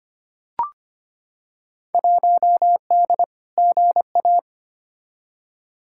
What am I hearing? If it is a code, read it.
Morse code, heard as a keyed tone, about 25 wpm, 700 Hz. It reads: 1D GA
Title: Field Day Exchanges - 25wpm